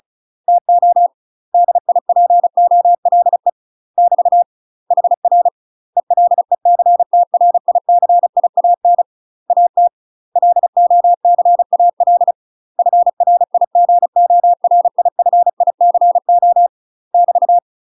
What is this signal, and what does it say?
Morse code, heard as a keyed tone, about 34 wpm, 700 Hz. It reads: TO DIPOLE <BT> HR ELECTRICIAN AT LOCAL FRIGORIFICO <BT>
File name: UK MASTER QSOs TRAINING/35WPM